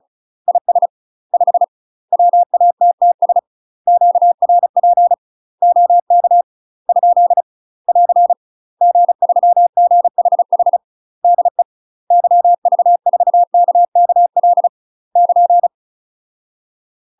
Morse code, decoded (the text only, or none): IS 5 WATTS QRP OK ? <AR> G3GHH DE YV4KKL <KN>